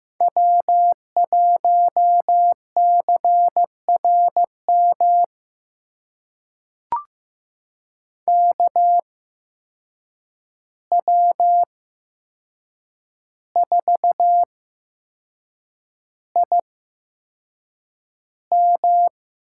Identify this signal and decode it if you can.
Morse code, heard as a keyed tone, about 15 wpm, spaced as for 6 wpm, 700 Hz. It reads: W1CRM KW4IM
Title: USA Call Signs - 5 Characters - 8x Character Spacing - 15wpm